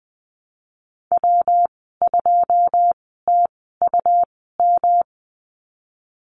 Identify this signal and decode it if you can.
Morse code, heard as a keyed tone, about 20 wpm, spaced as for 15 wpm, 700 Hz. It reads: W2TUM
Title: 179 Lesson 29g - Slash - Call Signs 20wpm